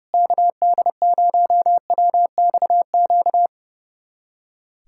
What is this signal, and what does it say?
Morse code, heard as a keyed tone, about 29 wpm, 700 Hz. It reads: KD0WXQ
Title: USA Callsign DE Callsign - 6 Characters - 30wpm